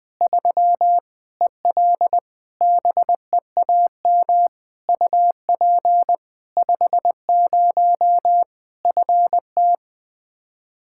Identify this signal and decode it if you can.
Morse code, heard as a keyed tone, about 21 wpm, 700 Hz. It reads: 3 EL BEAM UP 50 FT
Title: Practice QSO - Exchange 2 - Info - 20wpm